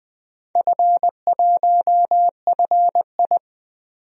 Morse code, heard as a keyed tone, about 20 wpm, 700 Hz. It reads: F1FI
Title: DXCC-100 Call Signs - 4 Characters - 20wpm